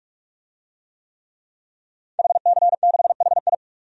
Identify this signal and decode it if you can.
Morse code, heard as a keyed tone, about 42 wpm, 700 Hz. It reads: HC6HI
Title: DXCC-100 Call Signs - 5 Characters - Repeated 2x - 45wpm